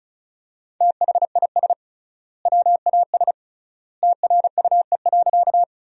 Morse code, heard as a keyed tone, about 36 wpm, 700 Hz. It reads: THIS WAS TRUE.
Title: 3x Word-Spacing - Sentences from Top 400 Words - 35wpm